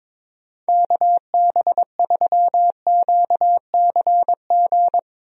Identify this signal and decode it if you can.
Morse code, heard as a keyed tone, about 22 wpm, 700 Hz. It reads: KB3QCG
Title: USA Call Signs - 6 Characters - Repeated 3x - 22wpm